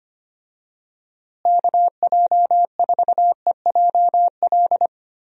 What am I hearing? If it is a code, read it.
Morse code, heard as a keyed tone, about 25 wpm, 700 Hz. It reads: KJ4EJL
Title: USA Call Signs - 6 Characters - Repeated 2x - 25wpm